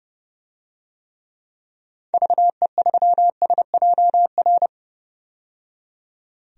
Morse code, heard as a keyed tone, about 30 wpm, 700 Hz. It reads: VE3SJR
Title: DXCC-100 Call Signs - 6 Characters - 30wpm